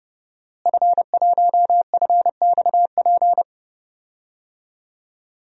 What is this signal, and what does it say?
Morse code, heard as a keyed tone, about 32 wpm, 700 Hz. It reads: F1FXP
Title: DXCC-100 Call Signs - 5 Characters - Repeated 3x - 30wpm